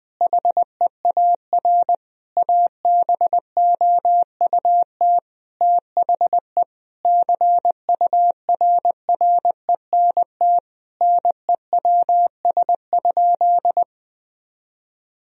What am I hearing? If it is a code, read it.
Morse code, heard as a keyed tone, about 21 wpm, 700 Hz. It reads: HEAR ABOUT THE CURRENT NEWS?
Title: Sentences from Top 1000 Words - Review Part 4 - 20wpm